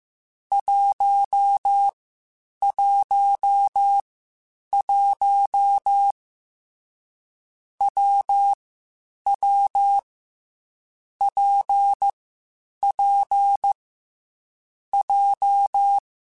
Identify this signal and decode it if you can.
Morse code, heard as a keyed tone, about 15 wpm, 800 Hz. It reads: 1 1 1 W W P P J